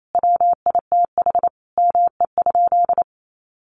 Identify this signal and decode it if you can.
Morse code, heard as a keyed tone, about 28 wpm, 700 Hz. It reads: WITH ME?
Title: Sentences from Top 300 Words 28wpm